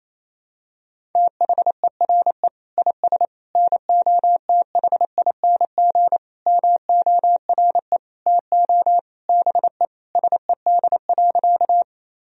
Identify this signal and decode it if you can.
Morse code, heard as a keyed tone, about 28 wpm, 700 Hz. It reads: THERE IS NOTHING MORE TO BE SED.
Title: Sentences from Top 500 Words and Abbreviations 28wpm